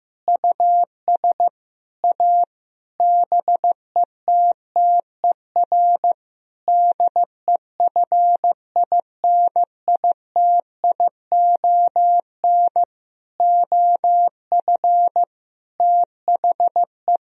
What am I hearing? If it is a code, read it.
Morse code, heard as a keyed tone, about 15 wpm, 700 Hz. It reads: US A BETTER DEFINITION OF THE
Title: Sentences from Top 1000 Words - Review Part 4 - 15wpm